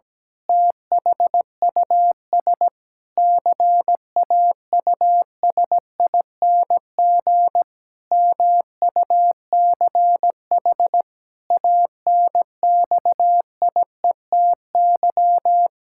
Morse code, heard as a keyed tone, about 17 wpm, 700 Hz. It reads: THUS CAUSING MUCH ANXIETY
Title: Pangram Challenge v2 - 17wpm